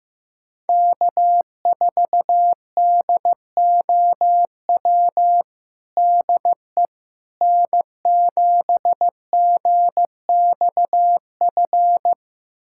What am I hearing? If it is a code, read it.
Morse code, heard as a keyed tone, about 15 wpm, 700 Hz. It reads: K4DOW DE N7GXF
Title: USA Callsign DE Callsign - 5 Characters - 15wpm